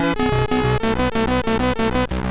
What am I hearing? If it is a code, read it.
There is no code.